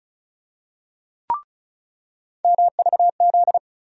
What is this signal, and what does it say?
Morse code, heard as a keyed tone, about 34 wpm, 700 Hz. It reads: MVZ
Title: USA Call Sign Postfixes - 3 Letters 35wpm